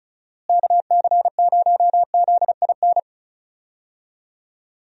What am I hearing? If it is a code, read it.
Morse code, heard as a keyed tone, about 36 wpm, 700 Hz. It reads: KC0ZIN
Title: USA Call Signs - 6 Characters - Repeated 3x - 35wpm